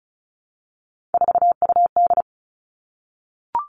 Morse code, heard as a keyed tone, about 34 wpm, 700 Hz. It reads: 4UD
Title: All Cut Numbers - 3-Digit 35wpm